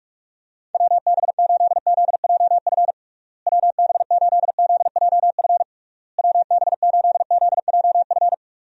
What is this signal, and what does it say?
Morse code, heard as a keyed tone, about 46 wpm, 700 Hz. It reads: WB8ZJF WB8ZJF WB8ZJF
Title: USA Call Signs - 6 Characters - Repeated 3x - 45wpm